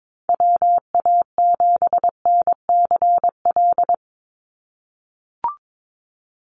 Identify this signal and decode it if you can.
Morse code, heard as a keyed tone, about 22 wpm, 700 Hz. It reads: WA7NCL
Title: NA SOTA Activator Callsigns 22wpm